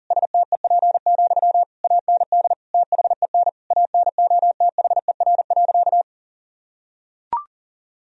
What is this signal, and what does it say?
Morse code, heard as a keyed tone, about 40 wpm, 700 Hz. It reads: STEP, AND THEN ANOTHER.
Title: Sentences from Top 1000 Words - Review Part 2 - 40wpm